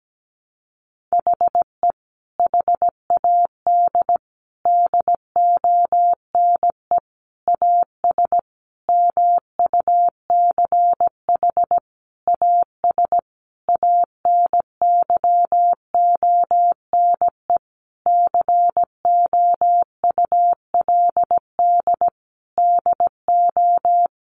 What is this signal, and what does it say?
Morse code, heard as a keyed tone, about 17 wpm, 700 Hz. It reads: HE HAD DONE AS MUCH AS ANYONE COULD DO
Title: Sentences from Top 1200 Words 17wpm